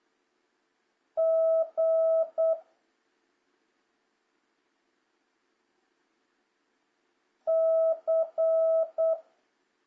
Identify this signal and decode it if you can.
Morse code, heard as a keyed tone, about 8 wpm, 650 Hz. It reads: G C
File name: jh8cbh / 2026cw